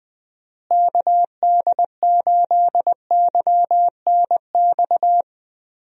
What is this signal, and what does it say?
Morse code, heard as a keyed tone, about 20 wpm, 700 Hz. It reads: KD8YNX
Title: USA Call Signs - 6 Characters - Repeated 2x - 20wpm